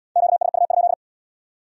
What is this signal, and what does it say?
Morse code, heard as a keyed tone, about 67 wpm, 700 Hz. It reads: QSU?